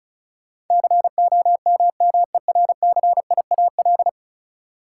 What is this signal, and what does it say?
Morse code, heard as a keyed tone, about 34 wpm, 700 Hz. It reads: COMMERCIAL
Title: Sentences from Top 1000 Words - Review Part 5 - New Word Drill - 35wpm